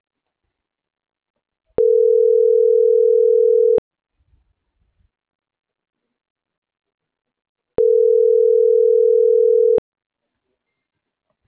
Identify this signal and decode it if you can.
Ringback tone (North America), 440 + 480 Hz.